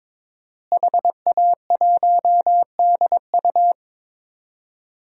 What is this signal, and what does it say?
Morse code, heard as a keyed tone, about 22 wpm, 700 Hz. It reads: HA1DU
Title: DXCC-100 Call Signs - 5 Characters - Repeated 2x - 22wpm